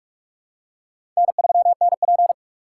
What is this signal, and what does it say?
Morse code, heard as a keyed tone, about 47 wpm, 700 Hz. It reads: N3NP
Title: USA Call Signs - 4 Characters - 3x Character Spacing - 45wpm